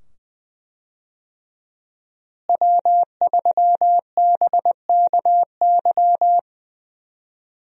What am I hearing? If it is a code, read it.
Morse code, heard as a keyed tone, about 20 wpm, 700 Hz. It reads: W3BKY
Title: USA Call Signs - 5 Characters - Repeated 2x - 20wpm